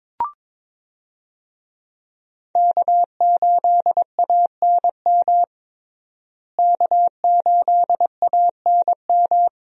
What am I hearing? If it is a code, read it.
Morse code, heard as a keyed tone, about 22 wpm, 700 Hz. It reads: K8ANM K8ANM
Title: USA Call Signs - 5 Characters - Repeated 3x - 22wpm